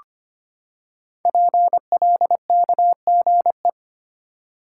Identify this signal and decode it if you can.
Morse code, heard as a keyed tone, about 25 wpm, 700 Hz. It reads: PLKGE
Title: Code Groups - 5 Letters - 25wpm